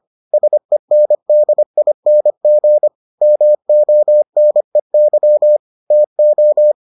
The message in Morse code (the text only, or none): SENDING MONEY TO